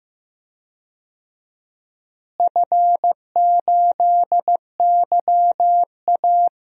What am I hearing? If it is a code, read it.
Morse code, heard as a keyed tone, about 15 wpm, 700 Hz. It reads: F8YA
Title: DXCC-100 Call Signs - 4 Characters - Repeated 2x - 15wpm